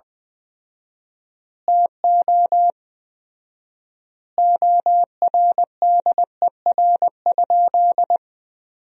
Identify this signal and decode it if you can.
Morse code, heard as a keyed tone, about 21 wpm, 700 Hz. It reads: TO ORDER?
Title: 4x Word-Spacing - Sentences from Top 400 Words - 20wpm